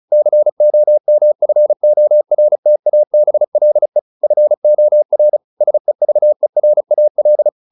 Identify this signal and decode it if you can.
Morse code, heard as a keyed tone, about 34 wpm, 600 Hz. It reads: COMFORTABLE FOR SEVERAL
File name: SS303 35